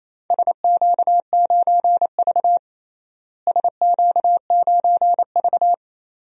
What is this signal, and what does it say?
Morse code, heard as a keyed tone, about 28 wpm, 700 Hz. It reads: SQ9V SQ9V
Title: DXCC-100 Call Signs - 4 Characters - Repeated 2x - 28wpm